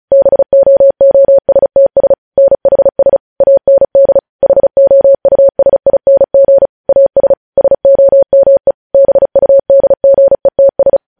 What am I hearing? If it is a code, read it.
Morse code, heard as a keyed tone, about 36 wpm, 550 Hz. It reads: BOOSTS NHS AND HOUSING AS SOME BUDGETS
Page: news headlines in morse code at 35 wpm